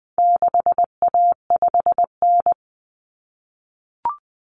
Morse code, heard as a keyed tone, about 20 wpm, 700 Hz. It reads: 6A5N
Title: Common Cut Numbers - 4-Digit 20wpm